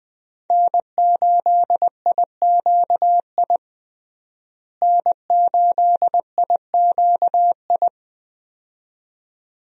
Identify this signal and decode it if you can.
Morse code, heard as a keyed tone, about 21 wpm, 700 Hz. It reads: N8IQI N8IQI
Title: USA Call Signs - 5 Characters - Repeated 2x - 20wpm